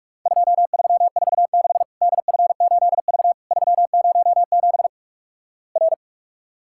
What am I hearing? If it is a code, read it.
Morse code, heard as a keyed tone, about 44 wpm, 700 Hz. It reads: 2346 DF8V 307 R
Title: Worked All Europe DX Contest - QTC Exchange - 45wpm